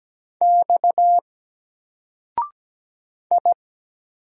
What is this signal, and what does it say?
Morse code, heard as a keyed tone, about 17 wpm, 700 Hz. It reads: X I
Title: Single Character (Letter:Number:Punctuation:Prosign) 17wpm